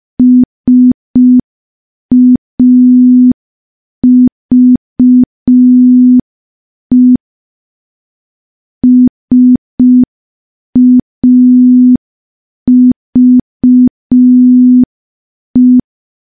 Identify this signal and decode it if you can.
Morse code, heard as a keyed tone, about 5 wpm, 250 Hz. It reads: SAVE SAVE
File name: morse / morse-training-05WPM